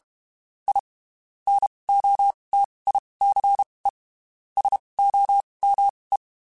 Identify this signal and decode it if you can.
Morse code, heard as a keyed tone, about 32 wpm, spaced as for 23 wpm, 800 Hz. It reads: I NOTICE SOME
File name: (spaced) / QSOs Spaced-mp3s / WD0BGP 1 @30 sp